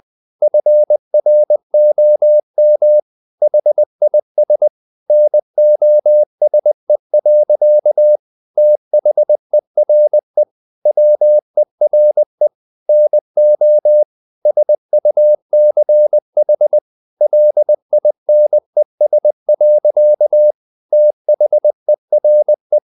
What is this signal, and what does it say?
Morse code, heard as a keyed tone, about 20 wpm, 600 Hz. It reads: FROM HIS NOSE. THERE WERE NO SUCH LINES. THERE
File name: the old man and the sea part 2 20wpm